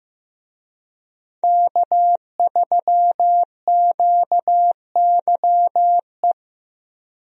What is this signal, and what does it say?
Morse code, heard as a keyed tone, about 15 wpm, 700 Hz. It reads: K3QYE
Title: USA Call Signs - 5 Characters - Repeated 3x - 15wpm